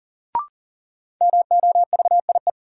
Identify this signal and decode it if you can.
Morse code, heard as a keyed tone, about 40 wpm, 700 Hz. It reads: MOVIE
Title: Sentences from Top 1000 Words - Review Part 5 - New Word Drill - 40wpm